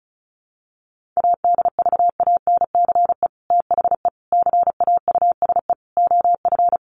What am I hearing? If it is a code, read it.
Morse code, heard as a keyed tone, about 34 wpm, 700 Hz. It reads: ADVANCE THE CAUSE OF
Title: Sentences from Top 1400 Words 35wpm